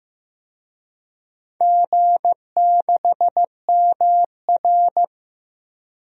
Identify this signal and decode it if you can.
Morse code, heard as a keyed tone, about 15 wpm, 700 Hz. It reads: G6MR
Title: DXCC-100 Call Signs - 4 Characters - Repeated 2x - 15wpm